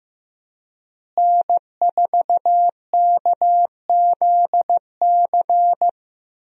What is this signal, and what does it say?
Morse code, heard as a keyed tone, about 15 wpm, 700 Hz. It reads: N4KZC
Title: USA Call Signs - 5 Characters - Repeated 3x - 15wpm